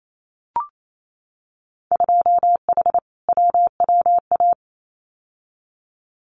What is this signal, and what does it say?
Morse code, heard as a keyed tone, about 28 wpm, 700 Hz. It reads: 2H WWA
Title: Winter Field Day Exchanges 28wpm